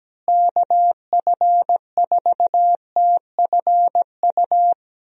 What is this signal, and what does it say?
Morse code, heard as a keyed tone, about 17 wpm, 700 Hz. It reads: KF4TFU
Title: USA Call Signs - 6 Characters - Repeated 3x - 17wpm